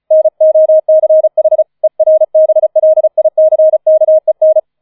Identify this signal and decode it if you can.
Morse code, heard as a keyed tone, about 34 wpm, 630 Hz. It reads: NOCH ERBLICKEN